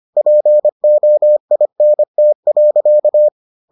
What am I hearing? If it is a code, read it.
Morse code, heard as a keyed tone, about 25 wpm, 600 Hz. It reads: POINT.